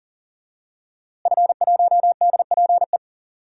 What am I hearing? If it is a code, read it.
Morse code, heard as a keyed tone, about 40 wpm, 700 Hz. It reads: F1DPE
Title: DXCC-100 Call Signs - 5 Characters - Repeated 3x - 40wpm